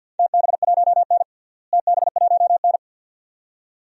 Morse code, heard as a keyed tone, about 48 wpm, 700 Hz. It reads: TB1N TB1N
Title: DXCC-100 Call Signs - 4 Characters - Repeated 2x - 50wpm